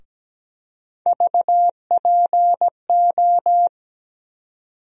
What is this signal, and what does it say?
Morse code, heard as a keyed tone, about 17 wpm, 700 Hz. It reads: VPO
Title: USA Call Sign Postfixes - 3 Letters 17wpm